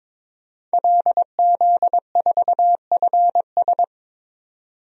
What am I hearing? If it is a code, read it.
Morse code, heard as a keyed tone, about 22 wpm, 700 Hz. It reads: LZ4FS
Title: DXCC-100 Call Signs - 5 Characters - Repeated 3x - 22wpm